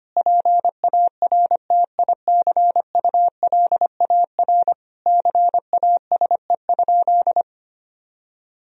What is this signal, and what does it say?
Morse code, heard as a keyed tone, about 25 wpm, 700 Hz. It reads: PARTICULAR CASE?
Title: Sentences from Top 1000 Words - Review Part 4 - 25wpm